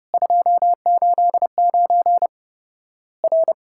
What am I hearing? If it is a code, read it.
Morse code, heard as a keyed tone, about 32 wpm, 700 Hz. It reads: 289 R
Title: Worked All Europe DX Contest - QTC Exchange - 30wpm